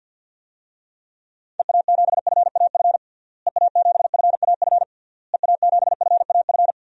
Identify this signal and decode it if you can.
Morse code, heard as a keyed tone, about 50 wpm, 700 Hz. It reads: EA7FAF EA7FAF EA7FAF
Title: DXCC-100 Call Signs - 6 Characters - Repeated 3x - 50wpm